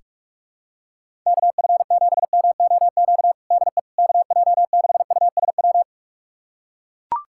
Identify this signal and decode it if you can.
Morse code, heard as a keyed tone, about 42 wpm, 700 Hz. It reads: KF7MOQ DE KJ6USW
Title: USA Callsign DE Callsign - 6 Characters - 45wpm